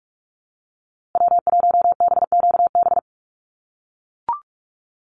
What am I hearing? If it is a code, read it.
Morse code, heard as a keyed tone, about 44 wpm, 700 Hz. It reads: W1BQB
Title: USA Call Signs Repeated 2x 45wpm